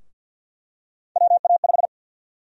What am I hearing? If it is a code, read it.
Morse code, heard as a keyed tone, about 46 wpm, 700 Hz. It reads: WA5
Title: USA Call Sign Prefixes - Repeated 2x - 50wpm